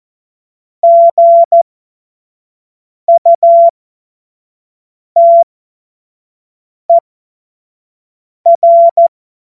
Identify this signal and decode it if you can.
Morse code, heard as a keyed tone, about 14 wpm, spaced as for 5 wpm, 680 Hz. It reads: GUTER
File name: Morsekurs / Witze / Witz 24 Juni 2025 25 F70